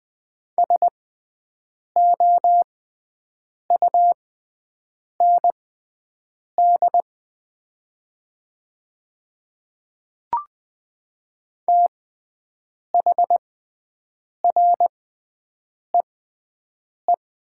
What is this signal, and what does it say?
Morse code, heard as a keyed tone, about 20 wpm, spaced as for 7 wpm, 700 Hz. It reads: SOUND THREE
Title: 6x Character Spacing - Top 200 Words - 20wpm